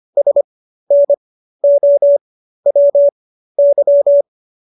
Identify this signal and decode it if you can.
Morse code, heard as a keyed tone, about 25 wpm, spaced as for 13 wpm, 570 Hz. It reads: SNOWY